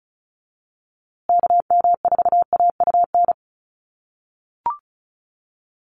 Morse code, heard as a keyed tone, about 34 wpm, 700 Hz. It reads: KM4AUN
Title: USA Call Signs Repeated 2x 35wpm